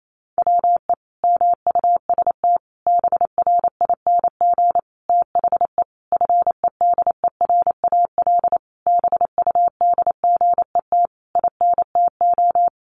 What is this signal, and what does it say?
Morse code, heard as a keyed tone, about 28 wpm, 700 Hz. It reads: WE MUST BRING THE FEDERAL BUDGET INTO
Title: Sentences from Top 1200 Words 28wpm